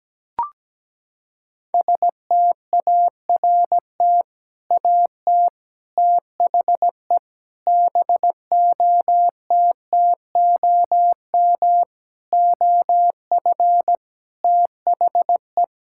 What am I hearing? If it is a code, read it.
Morse code, heard as a keyed tone, about 17 wpm, 700 Hz. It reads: START AT THE BOTTOM OF THE
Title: Sentences from Top 1000 Words - Review Part 6 - 17wpm